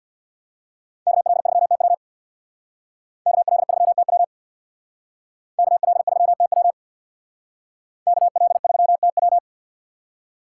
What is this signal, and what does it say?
Morse code, heard as a keyed tone, about 63 wpm, 690 Hz. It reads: KL3TW KL3TW KL3TW KL3TW